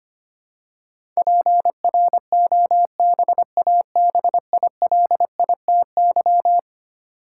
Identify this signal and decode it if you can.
Morse code, heard as a keyed tone, about 25 wpm, 700 Hz. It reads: PROBABILITY
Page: morse-code-ninja-practice-25wpm